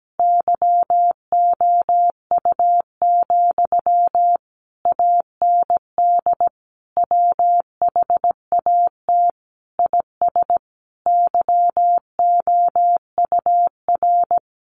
Morse code, heard as a keyed tone, about 17 wpm, 700 Hz. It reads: YOU, AND WHAT IS YOUR
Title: Sentences from Top 1200 Words 17wpm